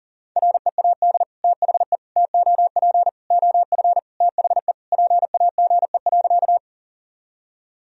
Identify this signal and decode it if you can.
Morse code, heard as a keyed tone, about 40 wpm, 700 Hz. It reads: READ THE TOP OF THE PAGE.
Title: Sentences from Top 1000 Words - Review Part 2 - 40wpm